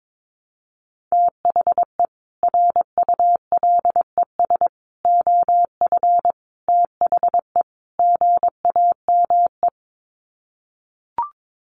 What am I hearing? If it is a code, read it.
Morse code, heard as a keyed tone, about 22 wpm, 700 Hz. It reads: THE RULES OF THE GAME
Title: Sets of 5 Words Encore2 22wpm